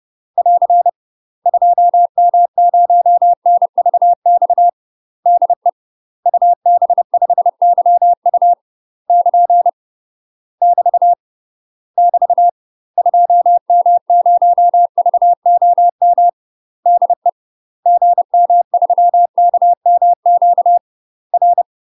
Morse code, heard as a keyed tone, about 30 wpm, 700 Hz. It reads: <AR> 2M0NVX DE UB5YU <KN> <BT> <BT> 2M0VOM DE GM3KMQ R
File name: UK MASTER QSOs TRAINING / 30WPM / UK MASTER QSOs 0036